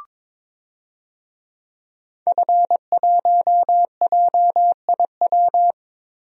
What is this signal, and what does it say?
Morse code, heard as a keyed tone, about 22 wpm, 700 Hz. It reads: F1JIW
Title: DXCC-100 Call Signs - 5 Characters - Repeated 3x - 22wpm